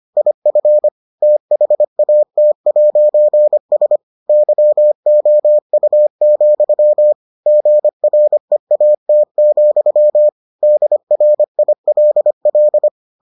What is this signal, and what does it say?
Morse code, heard as a keyed tone, about 25 wpm, 600 Hz. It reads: IF THAT'S YOU, GREAT, DRILL